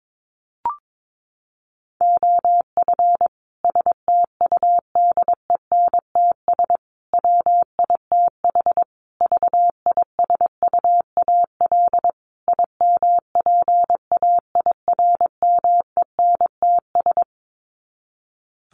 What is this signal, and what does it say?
Morse code, heard as a keyed tone, about 22 wpm, 700 Hz. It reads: OF STUDENTS WITH VISUAL IMPAIRMENTS
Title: Sets of 5 Words Encore 22wpm